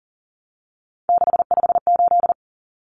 Morse code, heard as a keyed tone, about 40 wpm, 700 Hz. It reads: 658
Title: Numbers - 3-Digit 40wpm